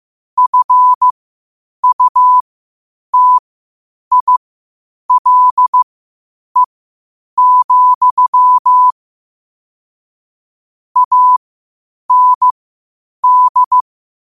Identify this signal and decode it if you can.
Morse code, heard as a keyed tone, about 15 wpm, spaced as for 8 wpm, 1000 Hz. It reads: FUTILE, AND
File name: qotdfiles / 8wpm